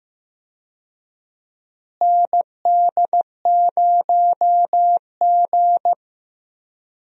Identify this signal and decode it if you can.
Morse code, heard as a keyed tone, about 15 wpm, 700 Hz. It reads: ND0G
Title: USA Call Signs - 4 Characters - Repeated 2x - 15wpm